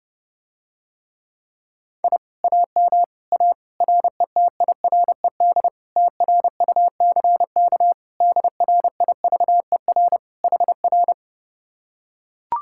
Morse code, heard as a keyed tone, about 32 wpm, 700 Hz. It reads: I AM A RETIRED TRUCK DRIVER HR